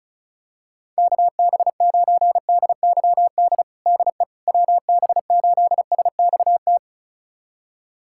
Morse code, heard as a keyed tone, about 36 wpm, 700 Hz. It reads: KB9DYD DE WB8SXT
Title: USA Callsign DE Callsign - 6 Characters - 35wpm